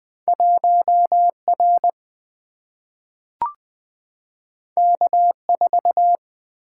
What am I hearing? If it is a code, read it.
Morse code, heard as a keyed tone, about 20 wpm, 700 Hz. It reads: 1 R K 4
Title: Single-Letter Single-Number 20wpm